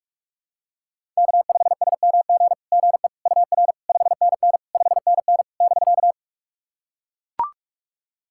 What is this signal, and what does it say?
Morse code, heard as a keyed tone, about 43 wpm, 700 Hz. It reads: K5SMG GE UR 5NN 5NN <BK>
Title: SOTA QSO Practice 45wpm